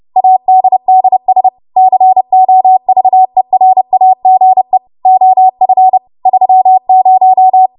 Morse code, heard as a keyed tone, about 30 wpm, 750 Hz. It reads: ADDS COVERAGE OF 30